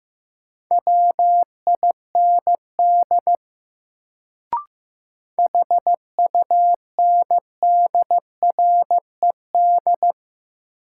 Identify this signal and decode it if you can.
Morse code, heard as a keyed tone, about 16 wpm, 700 Hz. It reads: WIND HUNDRED
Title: Sentences from Top 400 Words - New Word Drill - 15wpm